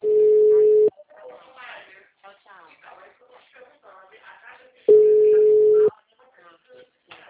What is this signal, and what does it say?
Ringback tone (Europe, 425 Hz), with speech.